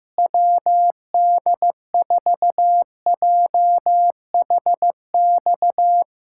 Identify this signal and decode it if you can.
Morse code, heard as a keyed tone, about 15 wpm, 700 Hz. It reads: WD4JHX
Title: USA Callsign DE Callsign - 6 Characters - 15wpm